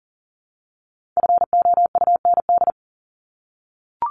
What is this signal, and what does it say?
Morse code, heard as a keyed tone, about 40 wpm, 700 Hz. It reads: FOUND
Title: Top 300 Words Repeated 40wpm